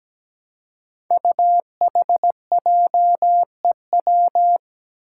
Morse code, heard as a keyed tone, about 17 wpm, 700 Hz. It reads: UHJEW
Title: Five Challenging Letter-Number Combinations 17wpm